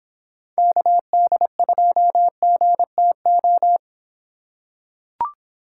Morse code, heard as a keyed tone, about 26 wpm, 700 Hz. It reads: KD2GTO